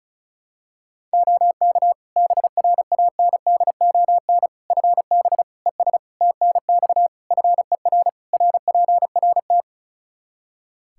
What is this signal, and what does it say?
Morse code, heard as a keyed tone, about 34 wpm, 700 Hz. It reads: OK BRANDON FB ES TNX FER RPRT
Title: K1USN SST QSO Practice 35wpm